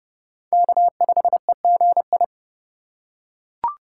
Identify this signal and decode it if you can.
Morse code, heard as a keyed tone, about 30 wpm, 700 Hz. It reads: K5EGI
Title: USA Call Signs - 5 Characters - 3x Character Spacing - 30wpm